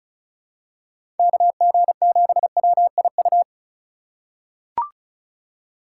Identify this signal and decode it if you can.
Morse code, heard as a keyed tone, about 36 wpm, 700 Hz. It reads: KG7WIU
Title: USA Call Signs - 6 Characters - 3x Character Spacing - 35wpm